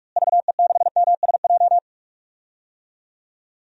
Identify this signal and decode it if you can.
Morse code, heard as a keyed tone, about 46 wpm, 700 Hz. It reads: VE6MSJ
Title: DXCC-100 Call Signs - 6 Characters - Repeated 3x - 45wpm